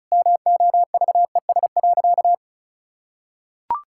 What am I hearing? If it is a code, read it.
Morse code, heard as a keyed tone, about 36 wpm, 700 Hz. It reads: MOVES.
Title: Pangram Challenge v2 - 35wpm